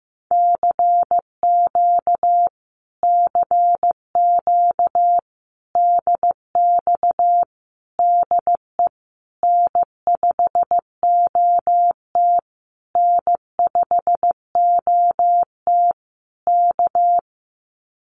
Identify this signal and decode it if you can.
Morse code, heard as a keyed tone, about 15 wpm, 700 Hz. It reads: CQ CQ DX DE N5OT N5OT K
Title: Common QSO Phrases Encore 15wpm